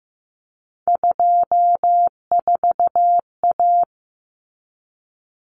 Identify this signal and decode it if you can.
Morse code, heard as a keyed tone, about 15 wpm, 700 Hz. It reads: 24A
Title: Common Cut Numbers - 3-Digit 15wpm